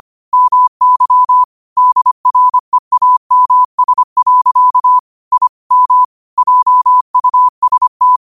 Morse code, heard as a keyed tone, about 25 wpm, 1000 Hz. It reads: MY DREAMS. I M JUST
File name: qotdfiles / 25wpm